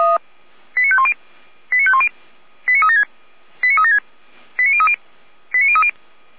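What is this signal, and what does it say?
Five-tone selective call (ZVEI), heard as quick runs of five tones, tones of about 70 ms. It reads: 89510 89510 89376 89376 89030 89030
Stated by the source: ZVEI five-tone call 89099 received via SDR.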